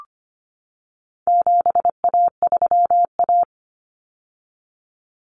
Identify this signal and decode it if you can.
Morse code, heard as a keyed tone, about 25 wpm, 700 Hz. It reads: 7A3A
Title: Common Cut Numbers - 4-Digit 25wpm